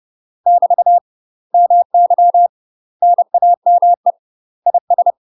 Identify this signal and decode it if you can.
Morse code, heard as a keyed tone, about 30 wpm, 700 Hz. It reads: <BT> MY NAME IS
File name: UK MASTER QSOs TRAINING / 30WPM / UK MASTER QSOs 0004